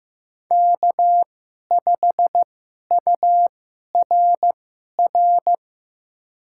Morse code, heard as a keyed tone, about 15 wpm, spaced as for 11 wpm, 700 Hz. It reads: K5URR
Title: USA Call Signs - 5 Characters - 2x Character Spacing - 15wpm